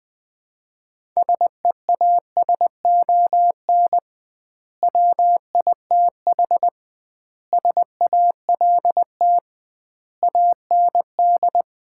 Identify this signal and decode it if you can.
Morse code, heard as a keyed tone, about 20 wpm, 700 Hz. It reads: SEASON WITH SALT AND
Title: 2x Word-Spacing - Sets of 5 Words - 20wpm